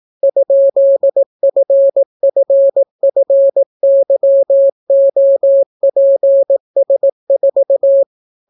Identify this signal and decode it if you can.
Morse code, heard as a keyed tone, about 18 wpm, 550 Hz. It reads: ?FFFYOPS4